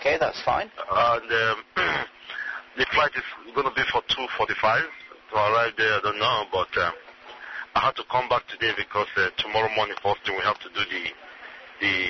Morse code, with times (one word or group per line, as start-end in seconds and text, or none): none